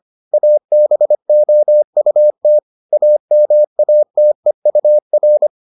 Morse code, heard as a keyed tone, about 25 wpm, 600 Hz. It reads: ABOUT AMATEUR